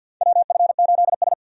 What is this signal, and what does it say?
Morse code, heard as a keyed tone, about 52 wpm, 700 Hz. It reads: WF8S
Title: USA Call Signs - 4 Characters - 50wpm